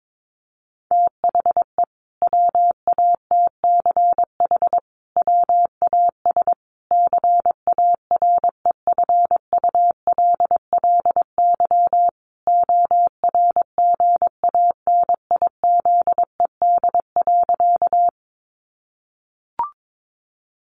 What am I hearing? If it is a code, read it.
Morse code, heard as a keyed tone, about 22 wpm, 700 Hz. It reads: THE WATCH WAS CAREFULLY ORGANIZED.
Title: Sentences from Top 1300 Words 22wpm